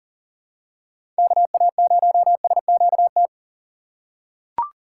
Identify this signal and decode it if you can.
Morse code, heard as a keyed tone, about 40 wpm, 700 Hz. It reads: KA0SQT
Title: USA Call Signs - 2x Character Spacing - 40wpm